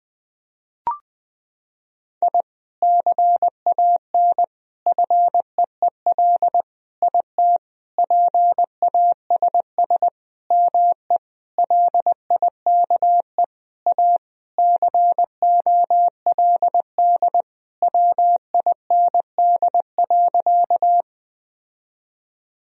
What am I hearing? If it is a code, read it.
Morse code, heard as a keyed tone, about 20 wpm, 700 Hz. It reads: I CAN FEEL IT PASS ME LIKE A COLD WIND.
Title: Sentences from Top 1000 Words - Review Part 2 - 20wpm